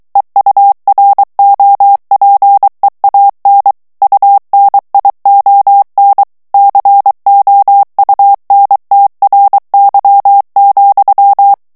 Morse code, heard as a keyed tone, about 23 wpm, 800 Hz. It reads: EUROPEAN UNION COUNTRY,
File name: morse mp3 / ANGLAIS texte clair 1 vitesse 22 mots minute